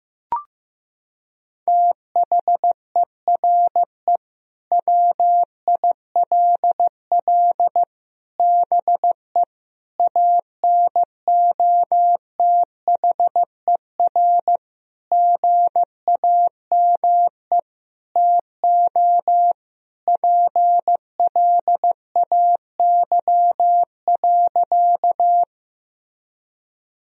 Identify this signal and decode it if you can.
Morse code, heard as a keyed tone, about 15 wpm, 700 Hz. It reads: THERE WILL BE ANOTHER GAME TO PLAY.
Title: Sentences from Top 1000 Words - Review Part 3 - 15wpm